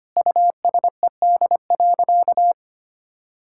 Morse code, heard as a keyed tone, about 25 wpm, 700 Hz. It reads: USED.
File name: practice / 25wpm / Sentences with 9-Word Phrases - 25wpm